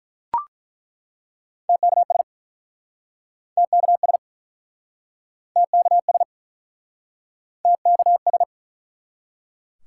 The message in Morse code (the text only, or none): TKS TKS TKS TKS